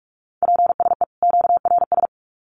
Morse code, heard as a keyed tone, about 44 wpm, 700 Hz. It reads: PSE QRS
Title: Common QSO Phrases 45wpm